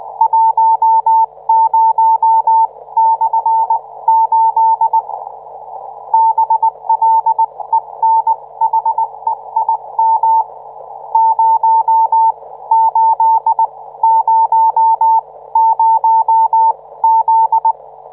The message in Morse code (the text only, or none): 10/8 BLENHEIM 0800Z